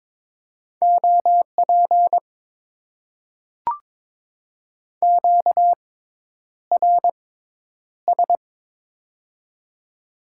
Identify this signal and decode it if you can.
Morse code, heard as a keyed tone, about 22 wpm, spaced as for 10 wpm, 700 Hz. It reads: OP QRS
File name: Top 90 QSO Elements - 6x Character Spacing - 22wpm